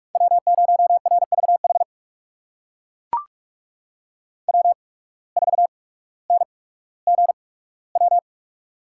Morse code, heard as a keyed tone, about 42 wpm, spaced as for 19 wpm, 700 Hz. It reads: W0RVH W4NGW
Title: USA Call Signs - 5 Characters - 8x Character Spacing - 45wpm